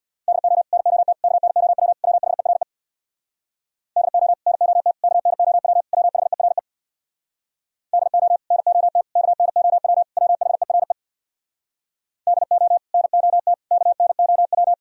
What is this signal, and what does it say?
Morse code, heard as a keyed tone, about 64 wpm, 700 Hz. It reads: DO NOT KNOW WHERE DO NOT KNOW WHERE DO NOT KNOW WHERE DO NOT KNOW